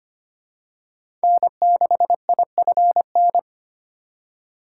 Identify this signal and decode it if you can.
Morse code, heard as a keyed tone, about 25 wpm, 700 Hz. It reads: N6IFN
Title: USA Call Signs - 5 Characters - 4x Character Spacing - 25wpm